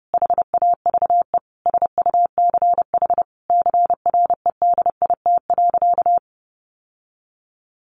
Morse code, heard as a keyed tone, about 30 wpm, 700 Hz. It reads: HAVE SUCH CREDIT.
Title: Sentences from Top 1000 Words 30wpm